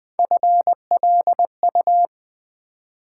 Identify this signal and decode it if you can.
Morse code, heard as a keyed tone, about 20 wpm, 700 Hz. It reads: FLU
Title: Build A Word - 1 Letter At A Time - 3-Letter Words - 20wpm